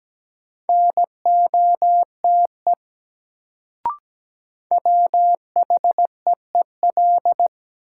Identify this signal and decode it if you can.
Morse code, heard as a keyed tone, about 17 wpm, 700 Hz. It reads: NOTE WHEEL
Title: Sentences from Top 500 Words - New Word Drill - 17wpm